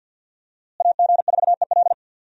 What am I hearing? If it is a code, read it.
Morse code, heard as a keyed tone, about 49 wpm, 700 Hz. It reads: AG4EL
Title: USA Call Signs - 5 Characters - Repeated 3x - 50wpm